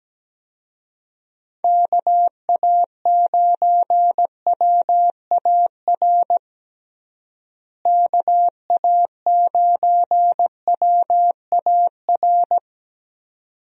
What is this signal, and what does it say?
Morse code, heard as a keyed tone, about 17 wpm, 700 Hz. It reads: KA9WAR KA9WAR
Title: USA Call Signs - 6 Characters - Repeated 2x - 17wpm